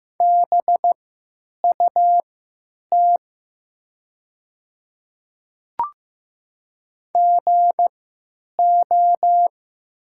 Morse code, heard as a keyed tone, about 15 wpm, spaced as for 9 wpm, 700 Hz. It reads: BUT GO
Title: 3x Character Spacing - Top 100 Words - 15wpm